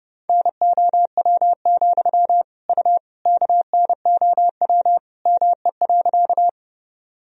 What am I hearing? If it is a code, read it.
Morse code, heard as a keyed tone, about 30 wpm, 700 Hz. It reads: NOW, U KNOW ME.
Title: Sentences from Top 200 Words and Abbreviations 30wpm